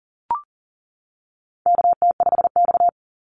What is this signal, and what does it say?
Morse code, heard as a keyed tone, about 40 wpm, 700 Hz. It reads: KT5X
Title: NA SOTA Activator Callsigns 40wpm